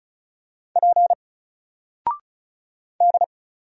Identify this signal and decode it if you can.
Morse code, heard as a keyed tone, about 34 wpm, 700 Hz. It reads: P D
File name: Single-Letter-Number-35wpm